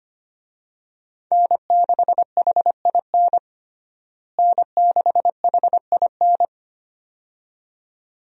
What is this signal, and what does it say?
Morse code, heard as a keyed tone, about 25 wpm, 700 Hz. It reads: N6HIN N6HIN
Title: USA Call Signs - 5 Characters - Repeated 2x - 25wpm